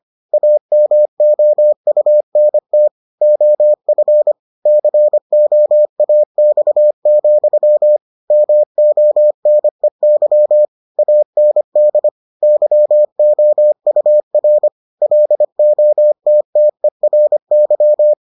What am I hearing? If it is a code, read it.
Morse code, heard as a keyed tone, about 25 wpm, 600 Hz. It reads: AMOUNT OF COAX, MONEY AND YOUR LOTTERY